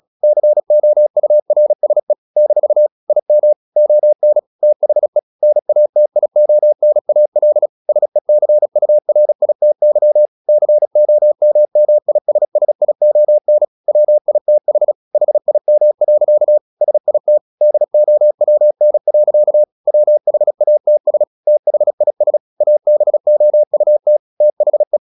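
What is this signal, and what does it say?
Morse code, heard as a keyed tone, about 36 wpm, 600 Hz. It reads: COURSE - IM ON THE NATIONAL SECURITY COMMISSION WITH HIM. SIT DOWN. WHATS THIS ABOUT THE
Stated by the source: Morse practice at 36 wpm, text of Ken Follett's novel Never, from CW med Gnister.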